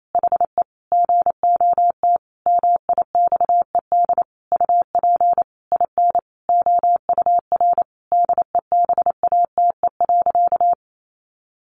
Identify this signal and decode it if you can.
Morse code, heard as a keyed tone, about 28 wpm, 700 Hz. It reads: HE GOT MIXED UP IN OUR DEBATE.
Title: Sentences from Top 1500 Words 28wpm